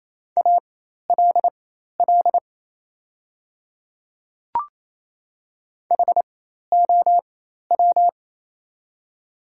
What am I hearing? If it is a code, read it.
Morse code, heard as a keyed tone, about 27 wpm, spaced as for 13 wpm, 700 Hz. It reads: ALL HOW
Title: 4x Character Spacing - Top 100 Words - 28wpm